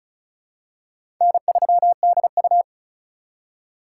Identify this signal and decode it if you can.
Morse code, heard as a keyed tone, about 36 wpm, 700 Hz. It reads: N3DU
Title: USA Call Signs - 4 Characters - 3x Character Spacing - 35wpm